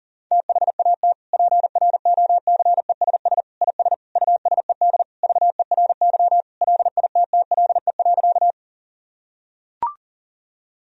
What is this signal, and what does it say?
Morse code, heard as a keyed tone, about 37 wpm, 700 Hz. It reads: THAT PROCESS IS USED VERY LITTLE.
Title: Sentences from Top 1000 Words - Review Part 3 - 40wpm